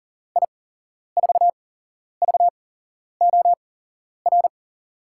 Morse code, heard as a keyed tone, about 40 wpm, spaced as for 11 wpm, 700 Hz. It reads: I4VOR
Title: USA Call Signs - 6 Characters - 8x Character Spacing - 40wpm